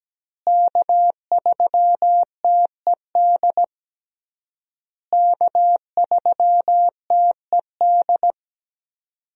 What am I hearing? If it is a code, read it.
Morse code, heard as a keyed tone, about 17 wpm, 700 Hz. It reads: K3TED K3TED
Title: USA Call Signs - 5 Characters - Repeated 3x - 17wpm